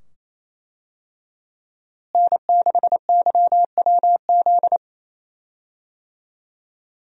Morse code, heard as a keyed tone, about 28 wpm, 700 Hz. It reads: N6YWZ
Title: USA Call Signs - 5 Characters - Repeated 2x - 28wpm